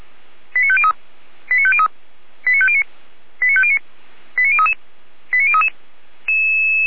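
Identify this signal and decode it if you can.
Five-tone selective call (ZVEI), heard as quick runs of five tones, tones of about 70 ms. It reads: 89503 89503 89509 89509 89033 89033
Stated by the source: ZVEI five-tone call received on SDR, from the audio archive of feuerwehr-orlamuende.org (fire brigade Orlamünde).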